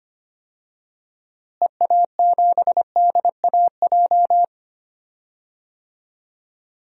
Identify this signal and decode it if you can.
Morse code, heard as a keyed tone, about 25 wpm, 700 Hz. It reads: EA7DAJ
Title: DXCC-100 Call Signs - 6 Characters - Repeated 2x - 25wpm